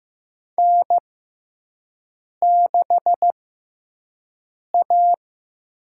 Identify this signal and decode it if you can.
Morse code, heard as a keyed tone, about 15 wpm, spaced as for 5 wpm, 700 Hz. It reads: N6A
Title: USA Call Signs - 6x Character Spacing - 15wpm